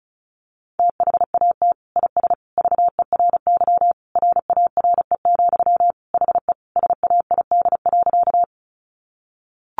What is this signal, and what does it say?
Morse code, heard as a keyed tone, about 34 wpm, 700 Hz. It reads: THAT IS VERY RARE, HE SAID.
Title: Sentences from Top 1100 Words 35wpm